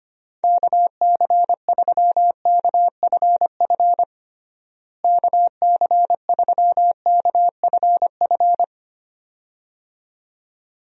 Morse code, heard as a keyed tone, about 25 wpm, 700 Hz. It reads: KC3KFF KC3KFF
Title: USA Call Signs - 6 Characters - Repeated 2x - 25wpm